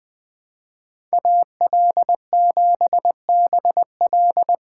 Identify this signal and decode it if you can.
Morse code, heard as a keyed tone, about 20 wpm, 700 Hz. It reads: AL7BL
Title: USA Call Signs - 5 Characters - 2x Character Spacing - 20wpm